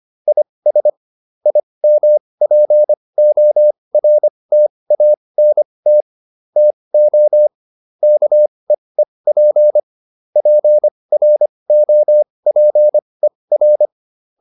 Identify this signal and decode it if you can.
Morse code, heard as a keyed tone, about 25 wpm, spaced as for 20 wpm, 600 Hz. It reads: IS IMPORTANT TO KEEP PROPER